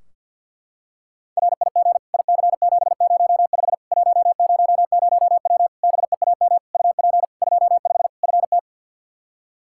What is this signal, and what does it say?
Morse code, heard as a keyed tone, about 48 wpm, 700 Hz. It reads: RIG IC705 100W BEAM UP 25 FT
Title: K1USN SST QSO Practice 50wpm